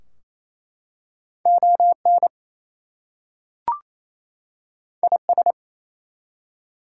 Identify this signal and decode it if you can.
Morse code, heard as a keyed tone, about 27 wpm, 700 Hz. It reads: ON IS
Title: Two Letter Words - 28wpm